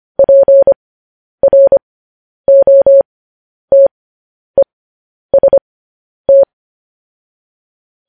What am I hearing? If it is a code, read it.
Morse code, heard as a keyed tone, about 25 wpm, spaced as for 10 wpm, 550 Hz. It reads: PROTEST
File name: podcast / News headlines-1x-25x10